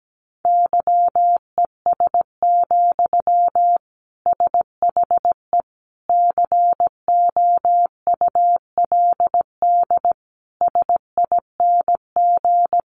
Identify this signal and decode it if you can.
Morse code, heard as a keyed tone, about 17 wpm, 700 Hz. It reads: YES, SHE COULD SING
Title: Sentences from Top 500 Words 17wpm